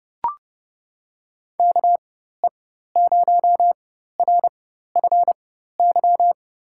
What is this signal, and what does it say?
Morse code, heard as a keyed tone, about 29 wpm, spaced as for 14 wpm, 700 Hz. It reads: KE0RFY
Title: USA Call Signs - 4x Character Spacing - 30wpm